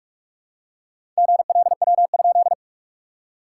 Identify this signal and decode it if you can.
Morse code, heard as a keyed tone, about 46 wpm, 660 Hz. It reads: GLW?
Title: SOTA QSO Practice 45wpm